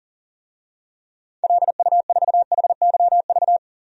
Morse code, heard as a keyed tone, about 37 wpm, 700 Hz. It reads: LU4HYV
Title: DXCC-100 Call Signs - 6 Characters - 40wpm